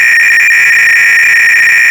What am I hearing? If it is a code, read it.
300-baud modem data.